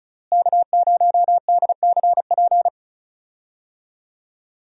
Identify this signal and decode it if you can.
Morse code, heard as a keyed tone, about 36 wpm, 700 Hz. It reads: K0DCP
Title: USA Call Signs - 5 Characters - Repeated 2x - 35wpm